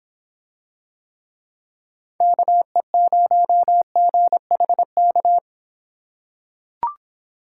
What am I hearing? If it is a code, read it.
Morse code, heard as a keyed tone, about 26 wpm, 700 Hz. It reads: KE0GHK